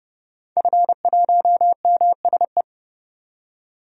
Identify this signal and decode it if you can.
Morse code, heard as a keyed tone, about 30 wpm, 700 Hz. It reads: F1MSE
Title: DXCC-100 Call Signs - 5 Characters - 30wpm